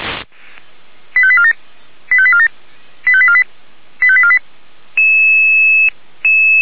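Five-tone selective call (ZVEI), heard as quick runs of five tones, tones of about 70 ms. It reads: 85847 85847 85848 85848
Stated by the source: ZVEI five-tone call received by SDR, logged by the fire brigade Feuerwehr Orlamünde.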